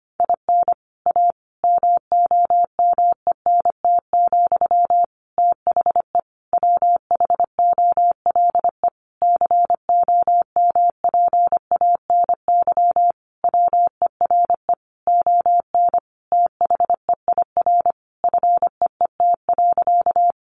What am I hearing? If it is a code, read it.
Morse code, heard as a keyed tone, about 25 wpm, 700 Hz. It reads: IN A MOMENT, THE WHOLE COMPANY WERE ON THEIR FEET.
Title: Sentences from Top 1000 Words 25wpm